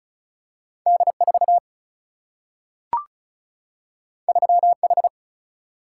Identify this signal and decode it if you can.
Morse code, heard as a keyed tone, about 36 wpm, 700 Hz. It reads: D 4 3 H
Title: Single-Letter Single-Number 35wpm